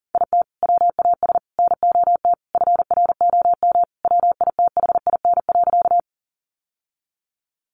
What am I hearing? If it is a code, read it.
Morse code, heard as a keyed tone, about 40 wpm, 700 Hz. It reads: IT WAS NOT FROM WITHIN.
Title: Sentences from Top 700 Words 40wpm